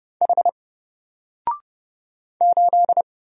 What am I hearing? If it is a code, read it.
Morse code, heard as a keyed tone, about 29 wpm, 700 Hz. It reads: H 8
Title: Single Letter-Number Alt 30wpm